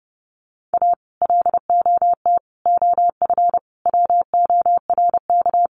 Morse code, heard as a keyed tone, about 30 wpm, 700 Hz. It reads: A LOT OF WORK
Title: Sets of 5 Words Encore2 30wpm